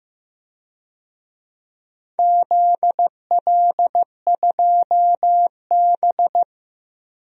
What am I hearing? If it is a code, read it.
Morse code, heard as a keyed tone, about 15 wpm, 700 Hz. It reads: ZL2B
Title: DXCC-100 Call Signs - 4 Characters - Repeated 2x - 15wpm